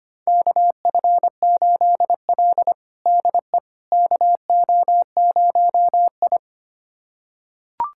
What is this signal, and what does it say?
Morse code, heard as a keyed tone, about 25 wpm, 700 Hz. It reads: KF8L DE KO0I
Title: USA Callsign DE Callsign - 4 Characters - 25wpm